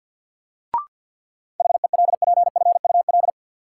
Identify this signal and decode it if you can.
Morse code, heard as a keyed tone, about 48 wpm, 700 Hz. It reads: HELPFUL
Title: Sentences from Top 1000 Words - Review Part 6 - New Word Drill - 50wpm